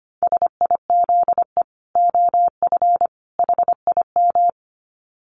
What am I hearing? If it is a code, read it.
Morse code, heard as a keyed tone, about 25 wpm, 700 Hz. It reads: SIZE OF HIM
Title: Dit Challenge 25wpm